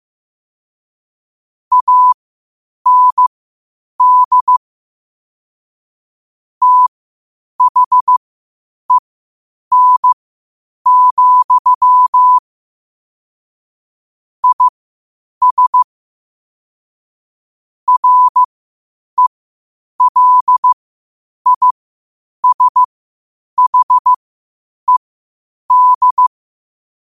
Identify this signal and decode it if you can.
Morse code, heard as a keyed tone, about 15 wpm, spaced as for 8 wpm, 1000 Hz. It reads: AND THEN, IS RELISHED
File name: qotdfiles / 8wpm